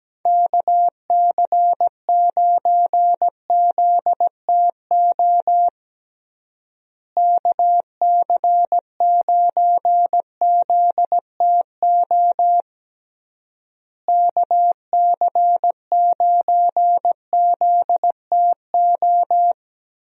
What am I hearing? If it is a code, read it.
Morse code, heard as a keyed tone, about 17 wpm, 700 Hz. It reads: KC9ZTO KC9ZTO KC9ZTO
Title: USA Call Signs - 6 Characters - Repeated 3x - 17wpm